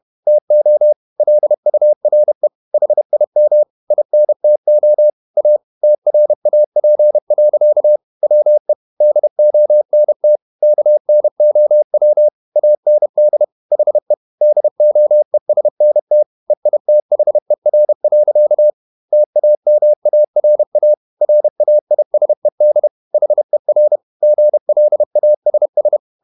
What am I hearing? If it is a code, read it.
Morse code, heard as a keyed tone, about 31 wpm, 600 Hz. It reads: TO LURE HIM INTO A TRAP. WE DONT KNOW AND HE DOESNT EITHER. TAMARA RAISED HER GLASS
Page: Never 31wpm | CW med Gnister